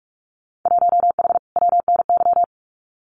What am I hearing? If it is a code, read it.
Morse code, heard as a keyed tone, about 46 wpm, 700 Hz. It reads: 1H WNY
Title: Winter Field Day Exchanges Encore 45wpm